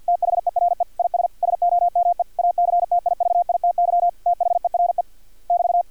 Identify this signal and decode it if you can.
Morse code, heard as a keyed tone, about 50 wpm, 700 Hz. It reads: THERE IS SOME ACTIVITY THERE <BT>
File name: archived / CRSnet